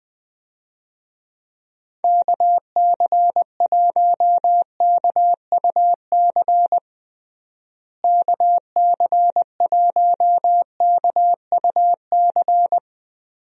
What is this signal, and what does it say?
Morse code, heard as a keyed tone, about 20 wpm, 700 Hz. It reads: KC1KUC KC1KUC
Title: USA Call Signs - 6 Characters - Repeated 3x - 20wpm